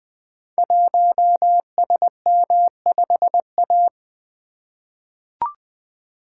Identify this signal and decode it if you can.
Morse code, heard as a keyed tone, about 20 wpm, 700 Hz. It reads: 1SM5A
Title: Code Groups - 5 Letter:Numbers - 20wpm